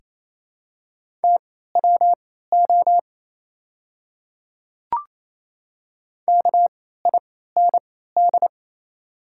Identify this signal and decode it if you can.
Morse code, heard as a keyed tone, about 27 wpm, spaced as for 16 wpm, 690 Hz. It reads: TWO KIND
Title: 3x Character Spacing - Top 300 Words - 28wpm